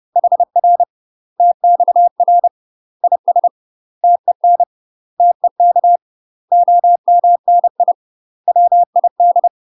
Morse code, heard as a keyed tone, about 30 wpm, 700 Hz. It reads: HR TXR IS TEN TEK OMNI WID
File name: UK MASTER QSOs TRAINING / 30WPM / UK MASTER QSOs 0046